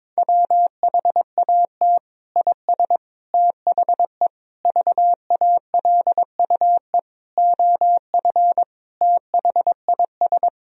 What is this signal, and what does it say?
Morse code, heard as a keyed tone, about 22 wpm, 700 Hz. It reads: WHAT IS THE VALUE OF THIS
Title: Sentences from Top 1000 Words - Review Part 3 - 22wpm